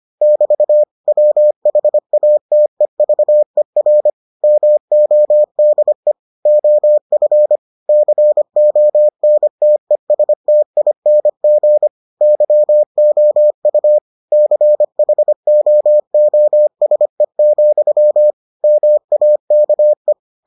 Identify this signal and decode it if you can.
Morse code, heard as a keyed tone, about 25 wpm, 600 Hz. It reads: <BT> WHATEVER MODE OF CONTESTING YOU CHOOSE, MAKE